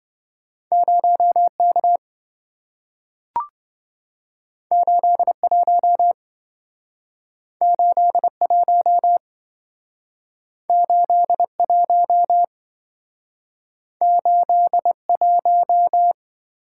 Morse code, heard as a keyed tone, about 27 wpm, 700 Hz. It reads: 0K 81 81 81 81